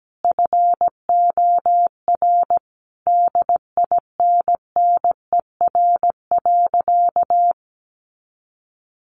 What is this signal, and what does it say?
Morse code, heard as a keyed tone, about 17 wpm, 700 Hz. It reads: FOR DINNER.
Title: Easy Phrases 17wpm